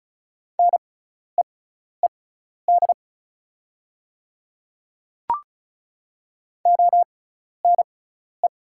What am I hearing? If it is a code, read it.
Morse code, heard as a keyed tone, about 33 wpm, spaced as for 12 wpm, 690 Hz. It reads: NEED ONE
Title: 6x Character Spacing - Top 200 Words - 35wpm